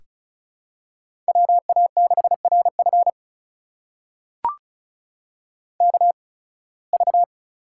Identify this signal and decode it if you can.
Morse code, heard as a keyed tone, about 34 wpm, spaced as for 23 wpm, 700 Hz. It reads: WA6RF KV
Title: USA Call Signs - 5 Characters - 8x Character Spacing - 35wpm